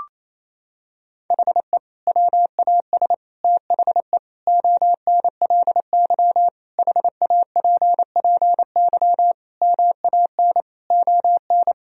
Morse code, heard as a keyed tone, about 28 wpm, 700 Hz. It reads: HE WAS THE ONLY HAPPY MAN ON
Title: Sentences from Top 1000 Words - Review Part 3 - 28wpm